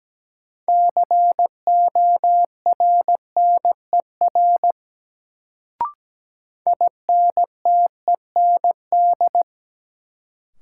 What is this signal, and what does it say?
Morse code, heard as a keyed tone, about 17 wpm, 700 Hz. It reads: CORNER INTEND
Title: Sentences from Top 1600 Words - New Word Drill Alt - 17wpm